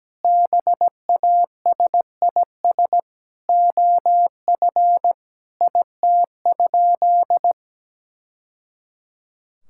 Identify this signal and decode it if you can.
Morse code, heard as a keyed tone, about 17 wpm, 700 Hz. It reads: BASIS OF IT?
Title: Sentences from Top 1000 Words - Review Part 5 - 17wpm